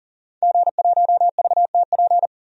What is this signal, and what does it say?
Morse code, heard as a keyed tone, about 38 wpm, 700 Hz. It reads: G1VTP
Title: DXCC-100 Call Signs - 5 Characters - 40wpm